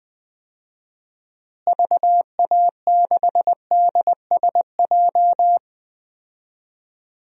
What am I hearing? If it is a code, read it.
Morse code, heard as a keyed tone, about 21 wpm, 700 Hz. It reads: VA6DSJ
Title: DXCC-100 Call Signs - 6 Characters - Repeated 2x - 20wpm